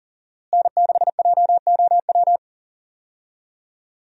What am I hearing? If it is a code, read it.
Morse code, heard as a keyed tone, about 40 wpm, 700 Hz. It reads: N6JOW
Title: USA Call Signs - 5 Characters - Repeated 2x - 40wpm